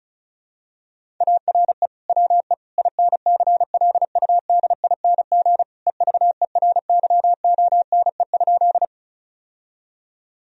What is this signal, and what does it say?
Morse code, heard as a keyed tone, about 36 wpm, 700 Hz. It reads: ARE WE INCLUDING EVERYONE?
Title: Sentences from Top 1000 Words - Review Part 4 - 35wpm